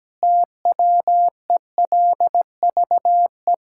Morse code, heard as a keyed tone, about 17 wpm, 700 Hz. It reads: TWELVE
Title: Pangram Challenge v2 - 17wpm